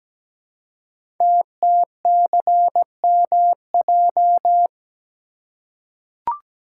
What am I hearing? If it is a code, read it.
Morse code, heard as a keyed tone, about 17 wpm, 700 Hz. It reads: TTCMJ
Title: Code Groups - 5 Letters - 17wpm